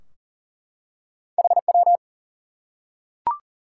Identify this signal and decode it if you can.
Morse code, heard as a keyed tone, about 40 wpm, 700 Hz. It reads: HW
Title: Top 90 QSO Elements 40wpm